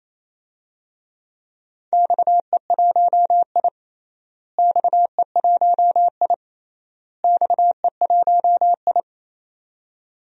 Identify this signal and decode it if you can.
Morse code, heard as a keyed tone, about 27 wpm, 700 Hz. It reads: XE1I XE1I XE1I
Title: DXCC-100 Call Signs - 4 Characters - Repeated 3x - 28wpm